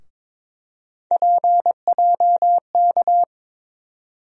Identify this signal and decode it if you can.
Morse code, heard as a keyed tone, about 22 wpm, 700 Hz. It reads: PJK
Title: USA Call Sign Postfixes - 3 Letters 22wpm